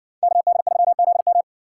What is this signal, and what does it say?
Morse code, heard as a keyed tone, about 55 wpm, 700 Hz. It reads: KD3ZM